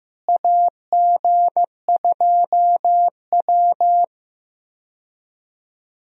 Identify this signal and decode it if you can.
Morse code, heard as a keyed tone, about 15 wpm, 700 Hz. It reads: AG2W
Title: USA Call Signs - 4 Characters - Repeated 3x - 15wpm